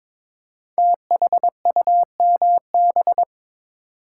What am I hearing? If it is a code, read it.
Morse code, heard as a keyed tone, about 22 wpm, 700 Hz. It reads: THUMB
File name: 5 Letter Words - First 3 Letters Given - 22wpm